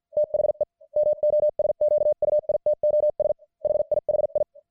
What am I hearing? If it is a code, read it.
Morse code, heard as a keyed tone, about 50 wpm, spaced as for 44 wpm, 600 Hz. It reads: THE MOSQUITOS HIHI
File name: archived / CWBros